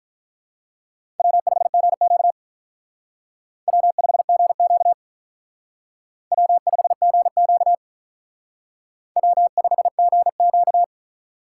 Morse code, heard as a keyed tone, about 43 wpm, 700 Hz. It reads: W5GQ W5GQ W5GQ W5GQ
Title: Speed Racing - USA 1x2 Call Signs 35wpm